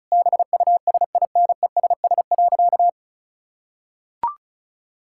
Morse code, heard as a keyed tone, about 34 wpm, 700 Hz. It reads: BUSINESS.
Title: Sentences from Top 1000 Words - Review Part 6 - 35wpm